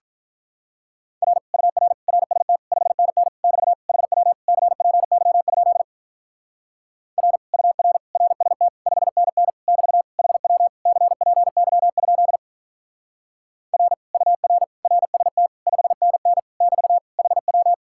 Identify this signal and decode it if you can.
Morse code, heard as a keyed tone, about 42 wpm, 700 Hz. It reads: R UR RST 5NN <BT> HW CPY? R UR RST 5NN <BT> HW CPY? R UR RST 5NN <BT> HW
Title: Speed-Racing Common QSO Phrases - 35wpm